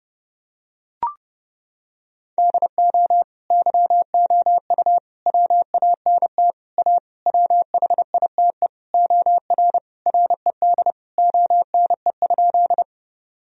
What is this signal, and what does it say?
Morse code, heard as a keyed tone, about 30 wpm, 700 Hz. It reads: DO YOU WANT A WHITE OR RED ONE?
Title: Sentences from Top 1000 Words - Review Part 2 - 30wpm